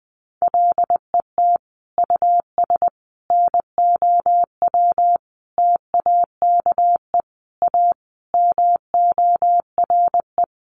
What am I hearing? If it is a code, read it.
Morse code, heard as a keyed tone, about 20 wpm, 700 Hz. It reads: LET US NOW TAKE A MORE
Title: Sentences from Top 900 Words 20wpm